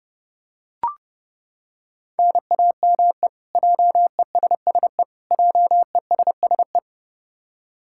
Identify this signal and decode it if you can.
Morse code, heard as a keyed tone, about 30 wpm, 700 Hz. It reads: NAME JESSE JESSE
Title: K1USN SST QSO Practice 30wpm